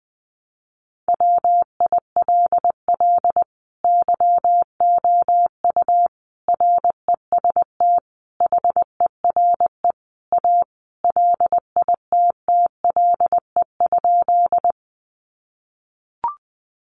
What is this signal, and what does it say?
Morse code, heard as a keyed tone, about 20 wpm, 700 Hz. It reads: WILL YOU REST HERE A LITTLE?
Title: Sentences from Top 500 Words 20wpm